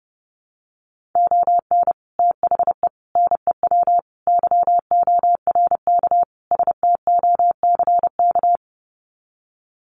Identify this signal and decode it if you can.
Morse code, heard as a keyed tone, about 30 wpm, 700 Hz. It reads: ON THE NEW YORK STOCK
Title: Sets of 5 Words Encore 30wpm